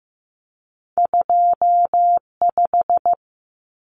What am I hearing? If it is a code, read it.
Morse code, heard as a keyed tone, about 16 wpm, 700 Hz. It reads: 25
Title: Numbers - 2-Digit 15wpm